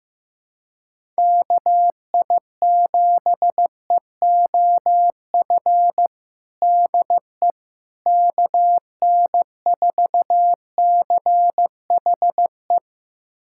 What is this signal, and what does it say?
Morse code, heard as a keyed tone, about 15 wpm, 700 Hz. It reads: KI7EOF DE KN4CHE
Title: USA Callsign DE Callsign - 6 Characters - 15wpm